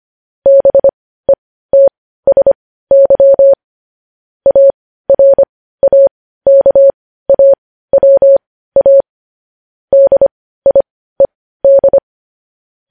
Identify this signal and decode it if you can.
Morse code, heard as a keyed tone, about 25 wpm, spaced as for 15 wpm, 550 Hz. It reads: BETSY ARAKAWA DIED
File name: podcast / News headlines-1x-25x15